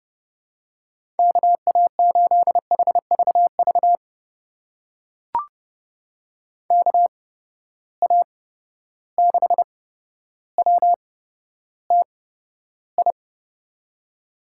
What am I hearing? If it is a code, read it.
Morse code, heard as a keyed tone, about 30 wpm, spaced as for 13 wpm, 700 Hz. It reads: KA8HVV KA6WTI
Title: USA Call Signs - 6 Characters - 8x Character Spacing - 30wpm